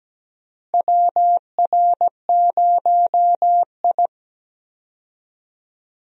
Morse code, heard as a keyed tone, about 17 wpm, 700 Hz. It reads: WR0I
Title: USA Call Signs - 4 Characters - Repeated 3x - 17wpm